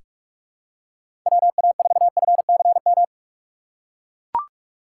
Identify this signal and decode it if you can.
Morse code, heard as a keyed tone, about 42 wpm, 700 Hz. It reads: WA4FCM
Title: USA Call Signs - 6 Characters - 6x Character Spacing - 45wpm